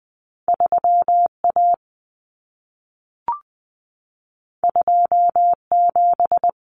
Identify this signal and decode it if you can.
Morse code, heard as a keyed tone, about 20 wpm, 700 Hz. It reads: 3A 27
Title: Common Cut Numbers - 2-Digit 20wpm